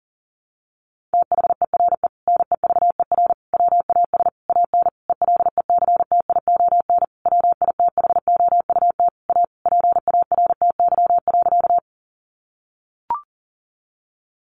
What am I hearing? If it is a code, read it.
Morse code, heard as a keyed tone, about 40 wpm, 700 Hz. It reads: THERE NEVER WAS AN ELECTION WITHOUT A PARTY.
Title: Sentences from Top 1500 Words 40wpm